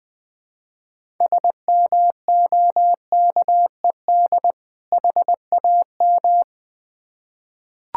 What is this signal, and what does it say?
Morse code, heard as a keyed tone, about 20 wpm, 700 Hz. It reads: SMOKED HAM
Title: Difficult Words with Hints - 20wpm